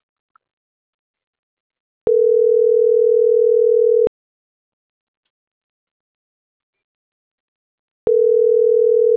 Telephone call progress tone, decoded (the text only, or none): ringback tone